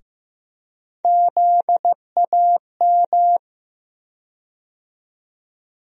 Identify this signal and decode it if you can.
Morse code, heard as a keyed tone, about 15 wpm, 700 Hz. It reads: ZAM
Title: USA Call Sign Postfixes - 3 Letters 15wpm